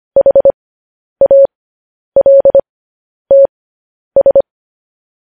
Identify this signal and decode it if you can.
Morse code, heard as a keyed tone, about 25 wpm, spaced as for 10 wpm, 550 Hz. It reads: HALTS
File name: podcast / News headlines-1x-25x10